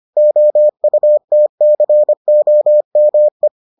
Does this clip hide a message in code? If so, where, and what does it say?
Yes: Morse code from 0.2 to 3.5 s, heard as a keyed tone, about 25 wpm, 600 Hz. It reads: OUTCOME